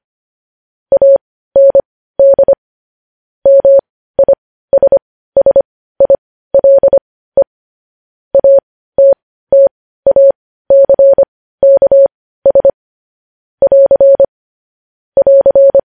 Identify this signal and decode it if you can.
Morse code, heard as a keyed tone, about 25 wpm, spaced as for 15 wpm, 550 Hz. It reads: AND MISSILE ATTACKS <AR> <AR>
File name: podcast / News headlines-1x-25x15